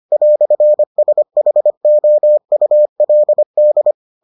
Morse code, heard as a keyed tone, about 25 wpm, 600 Hz. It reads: "SHOULD